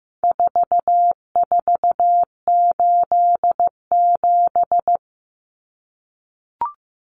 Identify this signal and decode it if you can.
Morse code, heard as a keyed tone, about 15 wpm, 700 Hz. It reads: 4487
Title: Numbers - 4-Digit 15wpm